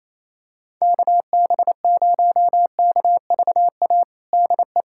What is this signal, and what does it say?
Morse code, heard as a keyed tone, about 28 wpm, 700 Hz. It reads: KB0KVA DE
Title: USA Callsign DE Callsign - 6 Characters - 28wpm